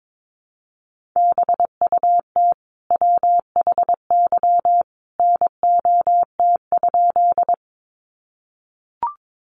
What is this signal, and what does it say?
Morse code, heard as a keyed tone, about 22 wpm, 700 Hz. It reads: BUT WHY NOT?
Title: Sentences from Top 600 Words 22wpm